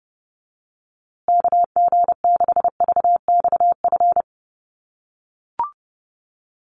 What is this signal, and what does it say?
Morse code, heard as a keyed tone, about 30 wpm, 700 Hz. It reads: KG6VXF
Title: USA Call Signs Repeated 2x 30wpm